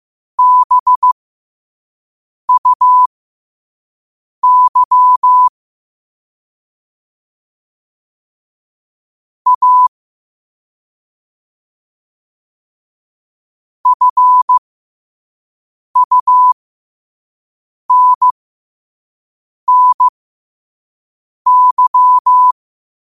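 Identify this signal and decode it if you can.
Morse code, heard as a keyed tone, about 15 wpm, spaced as for 5 wpm, 1000 Hz. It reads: BUY A FUNNY
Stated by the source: Morse code quote of the day sent at 5 wpm.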